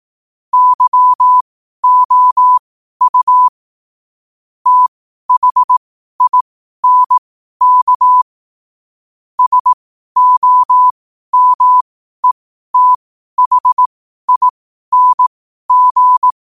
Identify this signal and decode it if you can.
Morse code, heard as a keyed tone, about 18 wpm, spaced as for 12 wpm, 1000 Hz. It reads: YOU THINK SOMETHING
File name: qotdfiles / 12wpm